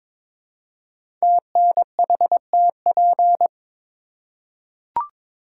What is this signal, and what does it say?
Morse code, heard as a keyed tone, about 22 wpm, 700 Hz. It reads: TNHTP
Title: Code Groups - 5 Letters - 22wpm